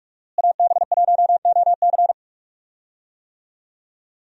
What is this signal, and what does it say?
Morse code, heard as a keyed tone, about 46 wpm, 700 Hz. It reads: AB1OC
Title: USA Call Signs - 5 Characters - Repeated 3x - 45wpm